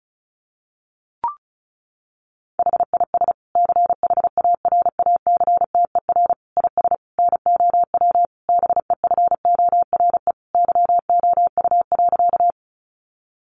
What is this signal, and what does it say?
Morse code, heard as a keyed tone, about 34 wpm, 700 Hz. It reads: HIS CHARACTER IS NOW BEFORE YOU.
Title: Sentences from Top 1200 Words 35wpm